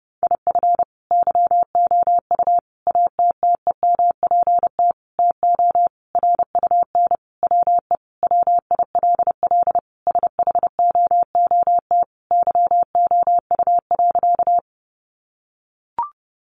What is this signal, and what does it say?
Morse code, heard as a keyed tone, about 30 wpm, 700 Hz. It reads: IF YOU ATTEMPT TO RUN WE WILL SHOOT YOU.
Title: Sentences from Top 1200 Words 30wpm